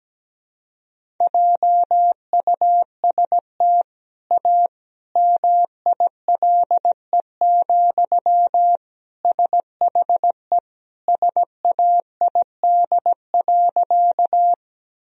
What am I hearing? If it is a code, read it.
Morse code, heard as a keyed tone, about 17 wpm, 700 Hz. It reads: JUST A MILE, SHE SAID.
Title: Sentences from Top 1000 Words - Review Part 1 - 17wpm